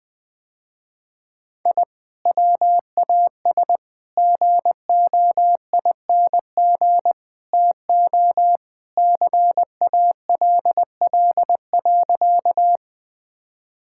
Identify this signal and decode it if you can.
Morse code, heard as a keyed tone, about 20 wpm, 700 Hz. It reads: I WAS GOING TO CALL.
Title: Sentences with 4-Word Phrases - Spoken Beginning - 20wpm